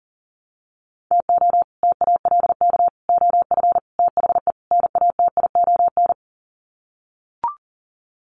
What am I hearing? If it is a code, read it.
Morse code, heard as a keyed tone, about 40 wpm, 700 Hz. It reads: TO TALK OF THE NATION
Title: Sets of 5 Words 40wpm